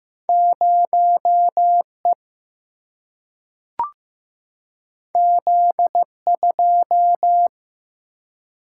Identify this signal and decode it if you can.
Morse code, heard as a keyed tone, about 15 wpm, 700 Hz. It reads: 0 E Z 2
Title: Single-Letter Single-Number 15wpm